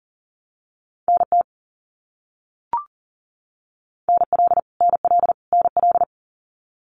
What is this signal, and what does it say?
Morse code, heard as a keyed tone, about 40 wpm, 700 Hz. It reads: NT NL NL NL
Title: Canadian Provinces and Territories - Repeated 3x - 40wpm